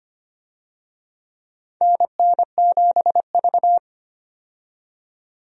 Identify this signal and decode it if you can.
Morse code, heard as a keyed tone, about 25 wpm, 700 Hz. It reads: NN7V
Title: USA Call Signs - 4 Characters - 25wpm